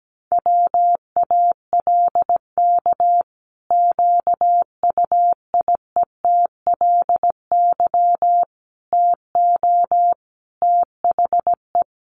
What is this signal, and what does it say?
Morse code, heard as a keyed tone, about 17 wpm, 700 Hz. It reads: WALK QUIETLY TO THE
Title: A Sound Of Thunder 17wpm